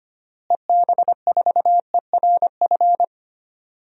Morse code, heard as a keyed tone, about 25 wpm, 700 Hz. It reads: EB4ERF
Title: DXCC-100 Call Signs - 6 Characters - Repeated 2x - 25wpm